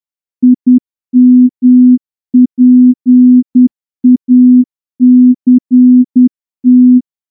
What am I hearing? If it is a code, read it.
Morse code, heard as a keyed tone, about 10 wpm, 250 Hz. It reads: IMPACT
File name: morse-training-10WPM